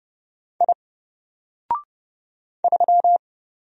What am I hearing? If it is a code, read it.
Morse code, heard as a keyed tone, about 30 wpm, 700 Hz. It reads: I 3
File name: Single-Letter-Number-30wpm